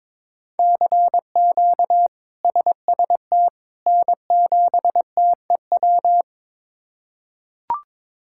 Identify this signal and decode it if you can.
Morse code, heard as a keyed tone, about 22 wpm, 700 Hz. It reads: CQ SST N7TEW
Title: K1USN SST QSO Practice - 22wpm